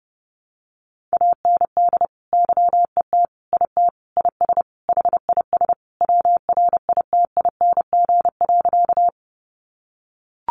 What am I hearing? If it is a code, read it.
Morse code, heard as a keyed tone, about 30 wpm, 700 Hz. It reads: AND YET IT IS HIS WRITING.
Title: Sentences from Top 800 Words 30wpm